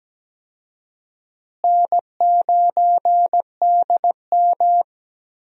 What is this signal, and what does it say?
Morse code, heard as a keyed tone, about 17 wpm, 700 Hz. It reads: N9DM
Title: USA Call Signs - 4 Characters - Repeated 2x - 17wpm